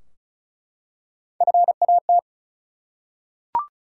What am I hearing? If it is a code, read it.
Morse code, heard as a keyed tone, about 36 wpm, 700 Hz. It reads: FAT
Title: Three Letter Words - 35wpm